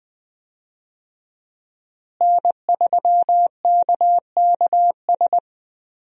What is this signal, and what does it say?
Morse code, heard as a keyed tone, about 21 wpm, 700 Hz. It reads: N3KKS
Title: USA Call Signs - 5 Characters - Repeated 3x - 20wpm